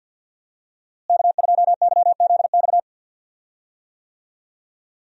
Morse code, heard as a keyed tone, about 46 wpm, 700 Hz. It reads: K2YZX
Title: USA Call Signs - 5 Characters - Repeated 3x - 50wpm